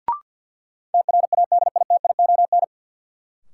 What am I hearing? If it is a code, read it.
Morse code, heard as a keyed tone, about 53 wpm, 700 Hz. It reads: TRADITION
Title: Sentences from Top 1500 Words - New Word Drill - 50wpm